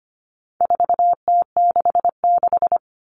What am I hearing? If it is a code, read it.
Morse code, heard as a keyed tone, about 25 wpm, 700 Hz. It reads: 4T66
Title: All Cut Numbers - 4-Digit 25wpm